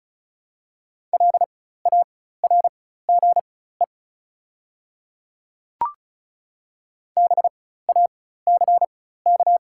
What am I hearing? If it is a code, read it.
Morse code, heard as a keyed tone, about 36 wpm, spaced as for 16 wpm, 690 Hz. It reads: LARGE BACK
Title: 4x Character Spacing - Top 300 Words - 35wpm